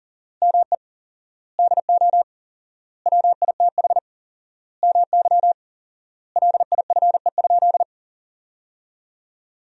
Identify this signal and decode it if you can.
Morse code, heard as a keyed tone, about 37 wpm, 700 Hz. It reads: ME DO WITH MY LIFE?
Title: 4x Word-Spacing - Sentences from Top 300 Words - 40wpm